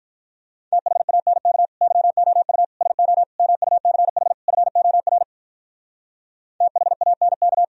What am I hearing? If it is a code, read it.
Morse code, heard as a keyed tone, about 46 wpm, 700 Hz. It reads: THANK YOU SO MUCH FOR THANK